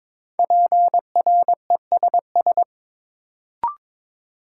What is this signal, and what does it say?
Morse code, heard as a keyed tone, about 22 wpm, 700 Hz. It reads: PRESS
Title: Sentences from Top 800 Words - New Word Drill - 22wpm